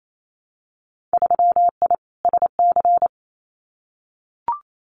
Morse code, heard as a keyed tone, about 28 wpm, 690 Hz. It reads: 3I SC
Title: Winter Field Day Exchanges Encore 28wpm